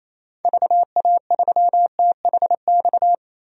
Morse code, heard as a keyed tone, about 28 wpm, 700 Hz. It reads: VA3THX
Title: DXCC-100 Call Signs - 6 Characters - 28wpm